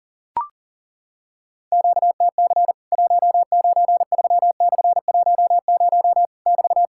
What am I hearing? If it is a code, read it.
Morse code, heard as a keyed tone, about 40 wpm, 700 Hz. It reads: QTC 193/10 <BT>
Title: Worked All Europe DX Contest - QTC Exchange - 40wpm